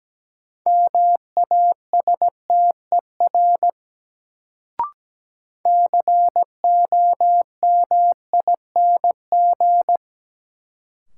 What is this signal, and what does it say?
Morse code, heard as a keyed tone, about 17 wpm, 700 Hz. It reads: MASTER COMING
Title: Sentences from Top 1300 Words - New Word Drill Alt - 17wpm